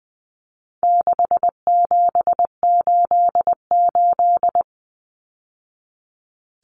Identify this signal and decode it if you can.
Morse code, heard as a keyed tone, about 20 wpm, 700 Hz. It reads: 6788
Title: Numbers - 4-Digit 20wpm